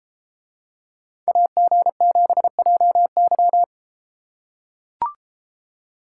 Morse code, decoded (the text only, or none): AG7JY